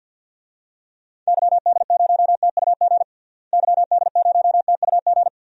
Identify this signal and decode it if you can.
Morse code, heard as a keyed tone, about 48 wpm, 700 Hz. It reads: YD0TUG YD0TUG
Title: DXCC-100 Call Signs - 6 Characters - Repeated 2x - 50wpm